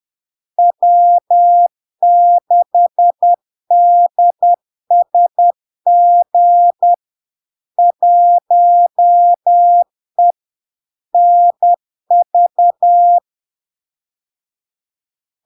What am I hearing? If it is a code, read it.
Morse code, heard as a keyed tone, about 10 wpm, 700 Hz. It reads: W6DSG 1E NV
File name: LARK-FD-2022-Log-in-CW-10wpm-700-Hz